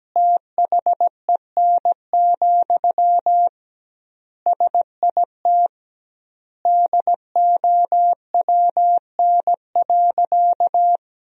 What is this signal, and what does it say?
Morse code, heard as a keyed tone, about 17 wpm, 700 Hz. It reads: THEN, SIT DOWN.
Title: 2x Word-Spacing - Sentences from Top 500 Words - 17wpm